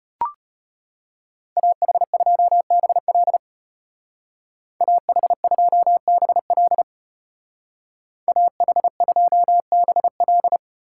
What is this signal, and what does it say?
Morse code, heard as a keyed tone, about 33 wpm, 700 Hz. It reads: AH2BL AH2BL AH2BL